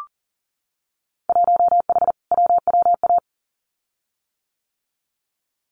Morse code, heard as a keyed tone, about 41 wpm, 700 Hz. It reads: 1H WWA
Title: Winter Field Day Exchanges 40wpm